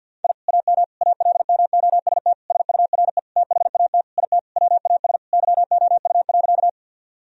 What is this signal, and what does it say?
Morse code, heard as a keyed tone, about 50 wpm, 700 Hz. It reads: I AM ALMOST SURE THAT IT WAS YOU.
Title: Sentences from Top 1000 Words - Review Part 4 - 50wpm